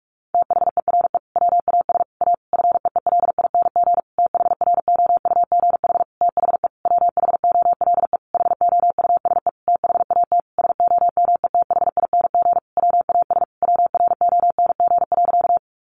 Morse code, heard as a keyed tone, about 44 wpm, 700 Hz. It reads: THERE WAS A FEELING THROUGH THE WHOLE HOUSE THAT SOMETHING WAS WRONG.
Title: Sentences from Top 1200 Words 45wpm